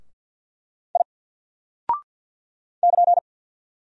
Morse code, heard as a keyed tone, about 49 wpm, 700 Hz. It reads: I <KN>
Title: Single Character (Letter:Number:Punctuation:Prosign) Alt 50wpm